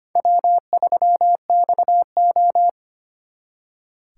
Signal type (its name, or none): Morse code